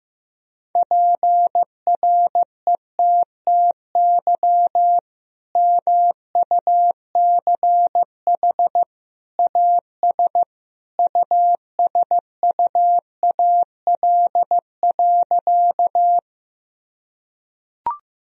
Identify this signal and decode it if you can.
Morse code, heard as a keyed tone, about 14 wpm, 700 Hz. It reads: PRETTY MUCH AS USUAL.
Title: Sentences from Top 1000 Words - Review Part 2 - 15wpm